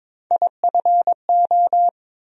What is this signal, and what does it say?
Morse code, heard as a keyed tone, about 22 wpm, 700 Hz. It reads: IFO
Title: USA Call Sign Postfixes - 3 Letters 22wpm